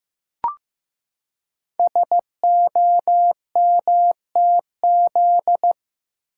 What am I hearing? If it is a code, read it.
Morse code, heard as a keyed tone, about 15 wpm, 700 Hz. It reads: SOMTZ
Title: Code Groups - 5 Letter:Numbers - 15wpm